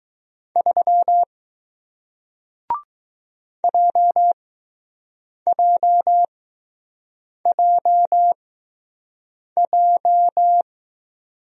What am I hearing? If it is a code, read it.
Morse code, heard as a keyed tone, about 20 wpm, spaced as for 6 wpm, 700 Hz. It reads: 3 JJJJ